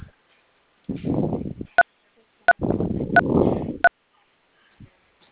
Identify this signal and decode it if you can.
DTMF, heard as telephone touch tones, tones of about 60 ms. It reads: 3333